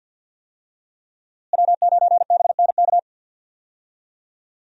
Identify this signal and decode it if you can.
Morse code, heard as a keyed tone, about 52 wpm, 700 Hz. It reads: W9BNK
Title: USA Call Signs - 5 Characters - Repeated 2x - 50wpm